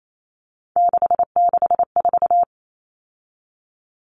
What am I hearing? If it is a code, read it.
Morse code, heard as a keyed tone, about 28 wpm, 700 Hz. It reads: 664
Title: Common Cut Numbers - 3-Digit 28wpm